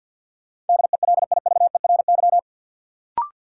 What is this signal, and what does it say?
Morse code, heard as a keyed tone, about 52 wpm, 700 Hz. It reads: DELIVERY
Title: Sentences from Top 1500 Words - New Word Drill Alt - 50wpm